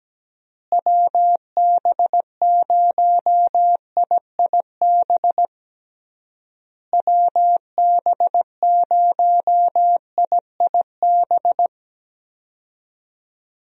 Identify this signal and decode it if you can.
Morse code, heard as a keyed tone, about 17 wpm, 700 Hz. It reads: WB0IIB WB0IIB
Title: USA Call Signs - 6 Characters - Repeated 3x - 17wpm